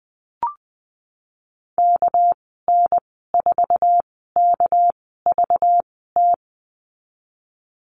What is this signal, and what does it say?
Morse code, heard as a keyed tone, about 20 wpm, spaced as for 14 wpm, 700 Hz. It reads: KN4KVT
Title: 207 Lesson 33g - Letter j - Call Signs 20wpm